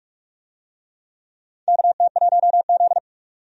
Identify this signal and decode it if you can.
Morse code, heard as a keyed tone, about 44 wpm, 700 Hz. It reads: KT1Z
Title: USA Call Signs - 4 Characters - 45wpm